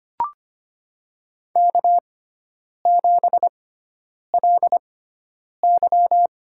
Morse code, heard as a keyed tone, about 25 wpm, spaced as for 9 wpm, 700 Hz. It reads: K7LY
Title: USA Call Signs - 4 Characters - 6x Character Spacing - 25wpm